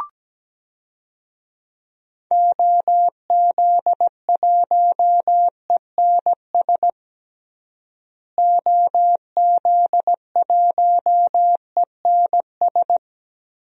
Morse code, heard as a keyed tone, about 17 wpm, 700 Hz. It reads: OZ1ENS OZ1ENS
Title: DXCC-100 Call Signs - 6 Characters - Repeated 2x - 17wpm